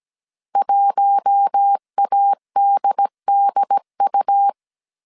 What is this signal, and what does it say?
Morse code, heard as a keyed tone, about 17 wpm, 800 Hz. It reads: 1ADDU
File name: CW / DET / 1 6 1500C S A 16wpm 2635